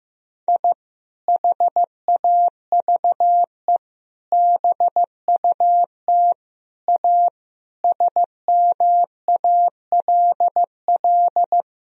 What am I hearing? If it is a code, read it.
Morse code, heard as a keyed tone, about 16 wpm, 700 Hz. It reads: I HAVE BUT A SMALL
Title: Sentences from Top 1000 Words - Review Part 4 - 15wpm